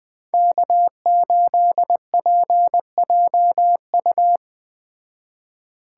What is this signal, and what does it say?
Morse code, heard as a keyed tone, about 20 wpm, 700 Hz. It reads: K8PJU